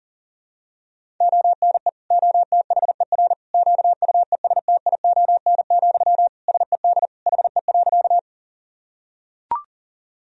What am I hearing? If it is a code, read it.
Morse code, heard as a keyed tone, about 40 wpm, 700 Hz. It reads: ONE OTHER QUESTION, SED HE.
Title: Sentences from Top 400 Words and Abbreviations 40wpm